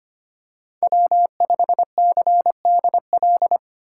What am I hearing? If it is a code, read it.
Morse code, heard as a keyed tone, about 25 wpm, 700 Hz. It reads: W5CDL
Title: USA Call Signs - 5 Characters - 4x Character Spacing - 25wpm